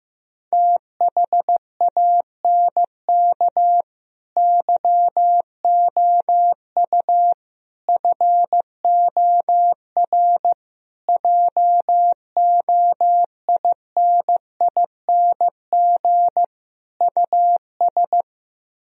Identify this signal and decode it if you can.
Morse code, heard as a keyed tone, about 15 wpm, 700 Hz. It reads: THANK YOU FOR JOINING US
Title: Speed-Racing - Sets of 5 Words 15wpm